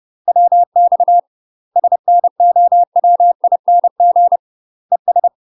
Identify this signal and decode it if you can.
Morse code, heard as a keyed tone, about 30 wpm, 700 Hz. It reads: WX SNOWING ES
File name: UK MASTER QSOs TRAINING / 30WPM / UK MASTER QSOs 0001